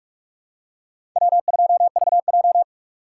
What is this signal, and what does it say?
Morse code, heard as a keyed tone, about 47 wpm, 700 Hz. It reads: W2VJ
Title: USA Call Signs - 4 Characters - 3x Character Spacing - 45wpm